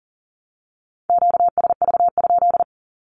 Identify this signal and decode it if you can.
Morse code, heard as a keyed tone, about 39 wpm, 700 Hz. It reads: QSV?
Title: Q-Codes 40wpm